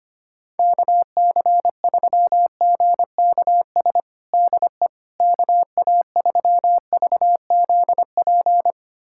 Morse code, heard as a keyed tone, about 25 wpm, 700 Hz. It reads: KC3GKS DE KA3VZP
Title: USA Callsign DE Callsign - 6 Characters - 25wpm